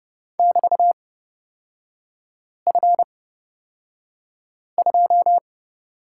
Morse code, heard as a keyed tone, about 30 wpm, spaced as for 5 wpm, 700 Hz. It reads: <BT>F2
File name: Lesson 36e - BT - Three Characters 30wpm